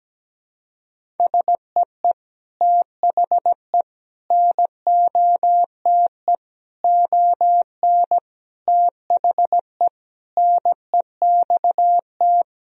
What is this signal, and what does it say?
Morse code, heard as a keyed tone, about 17 wpm, 700 Hz. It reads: SEE THE NOTE ON THE NEXT
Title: Sentences from Top 1000 Words - Review Part 2 - 17wpm